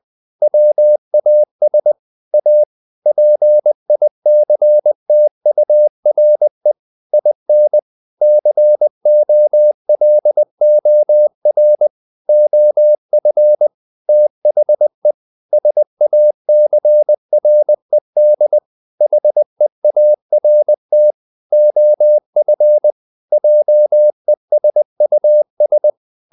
Morse code, heard as a keyed tone, about 20 wpm, 600 Hz. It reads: WAS A PICTURE IN COLOR OF THE SACRED HEART OF JESUS
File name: the old man and the sea / the old man and the sea part 1 20wpm